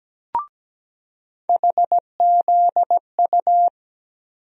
Morse code, heard as a keyed tone, about 17 wpm, 700 Hz. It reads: HZU
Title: USA Call Sign Postfixes - 3 Letters 17wpm